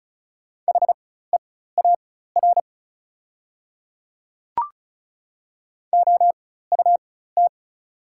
Morse code, heard as a keyed tone, about 34 wpm, spaced as for 16 wpm, 700 Hz. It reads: HEAR OUT
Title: 4x Character Spacing - Top 300 Words - 35wpm